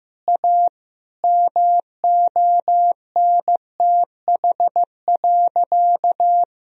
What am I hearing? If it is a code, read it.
Morse code, heard as a keyed tone, about 15 wpm, 700 Hz. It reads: A MONTH.
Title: Sentences from Top 1000 Words - Review Part 3 - 15wpm